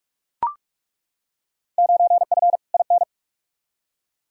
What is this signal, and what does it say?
Morse code, heard as a keyed tone, about 41 wpm, 700 Hz. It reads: 9F IN
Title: Field Day Exchanges Encore - 45wpm